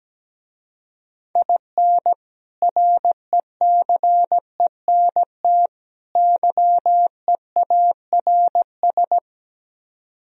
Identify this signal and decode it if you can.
Morse code, heard as a keyed tone, about 17 wpm, 700 Hz. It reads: IN RECENT YEARS
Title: Speed-Racing - Sets of 3 Words 17wpm